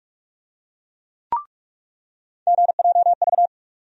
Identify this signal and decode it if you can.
Morse code, heard as a keyed tone, about 46 wpm, 700 Hz. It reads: GJV
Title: USA Call Sign Postfixes - 3 Letters 45wpm